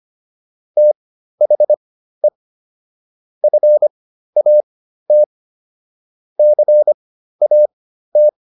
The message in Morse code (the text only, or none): THE FAT CAT